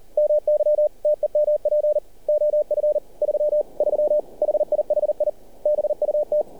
Morse code, heard as a keyed tone, about 38 wpm, 600 Hz. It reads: MY TEMP OF 33 HIHI BUT
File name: archived / CRSnet